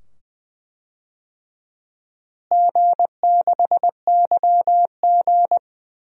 Morse code, heard as a keyed tone, about 20 wpm, 700 Hz. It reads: G6YG
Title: DXCC-100 Call Signs - 4 Characters - Repeated 2x - 20wpm